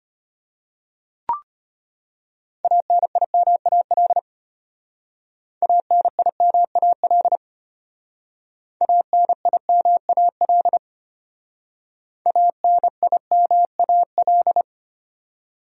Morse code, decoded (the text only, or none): ANIMAL ANIMAL ANIMAL ANIMAL